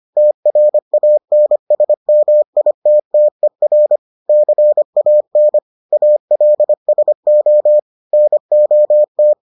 Morse code, heard as a keyed tone, about 25 wpm, 600 Hz. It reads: TRANSMITTER CAN ALSO NOT